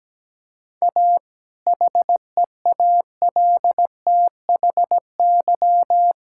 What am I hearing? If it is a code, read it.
Morse code, heard as a keyed tone, about 17 wpm, 700 Hz. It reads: A HEALTHY
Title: Sentences from Top 1000 Words - Review Part 5 - 17wpm